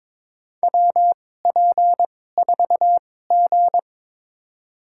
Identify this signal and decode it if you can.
Morse code, heard as a keyed tone, about 22 wpm, spaced as for 16 wpm, 700 Hz. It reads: WP4G
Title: USA Call Signs - 4 Characters - 2x Character Spacing - 22wpm